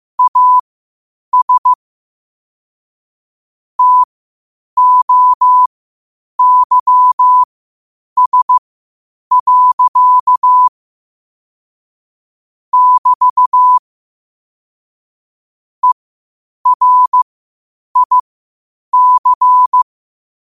Morse code, heard as a keyed tone, about 15 wpm, spaced as for 8 wpm, 1000 Hz. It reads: AS TOYS. <BT> ERIC
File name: qotdfiles / 8wpm